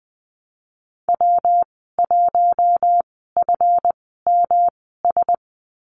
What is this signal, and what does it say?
Morse code, heard as a keyed tone, about 20 wpm, spaced as for 15 wpm, 700 Hz. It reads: W1FMS
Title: 186 Lesson 30g - Letter b - Call Signs 20wpm